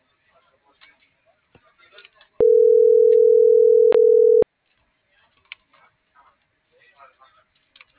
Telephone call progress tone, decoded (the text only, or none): ringback tone